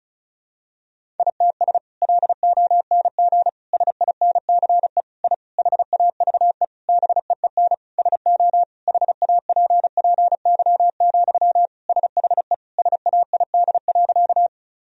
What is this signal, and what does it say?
Morse code, heard as a keyed tone, about 34 wpm, 700 Hz. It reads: ITS LONG SINCE I HAVE BEEN SO HAPPY, SHE SAID.
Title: Sentences from Top 1000 Words - Review Part 1 - 35wpm